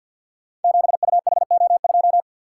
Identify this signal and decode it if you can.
Morse code, heard as a keyed tone, about 51 wpm, 700 Hz. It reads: 7UHO2
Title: Code Groups - 5 Letter:Numbers - 50wpm